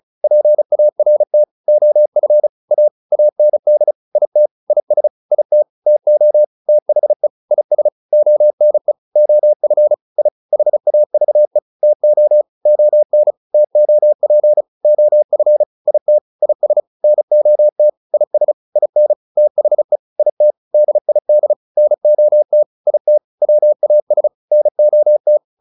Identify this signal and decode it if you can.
Morse code, heard as a keyed tone, about 34 wpm, 600 Hz. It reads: PART OF A AND IT IS IT TO THE IS ONE OF I HAVE TO ON TOP OF IT IS NOT IS IN THE IT DID NOT IT WAS NOT